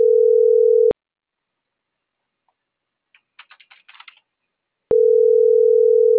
Ringback tone (North America), 440 + 480 Hz.